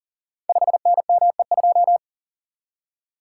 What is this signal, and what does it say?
Morse code, heard as a keyed tone, about 39 wpm, 700 Hz. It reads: 5NME2
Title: Code Groups - 5 Letter:Numbers - 40wpm